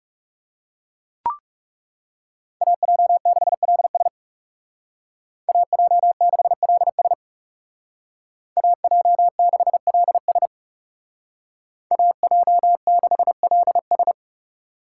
Morse code, heard as a keyed tone, about 39 wpm, 700 Hz. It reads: AJ6LS AJ6LS AJ6LS AJ6LS